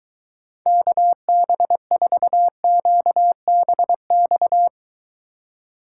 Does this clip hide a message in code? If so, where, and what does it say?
Yes: Morse code from 0.7 to 4.7 s, heard as a keyed tone, about 23 wpm, 700 Hz. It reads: KB4QBX